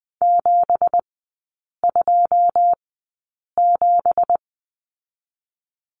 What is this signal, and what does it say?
Morse code, heard as a keyed tone, about 20 wpm, spaced as for 8 wpm, 700 Hz. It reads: 727
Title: 166 Lesson 27e - Number 7 - Three Characters 20wpm